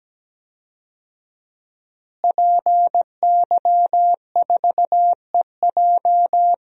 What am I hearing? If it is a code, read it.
Morse code, heard as a keyed tone, about 17 wpm, 700 Hz. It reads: PY4EJ
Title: DXCC-100 Call Signs - 5 Characters - Repeated 2x - 17wpm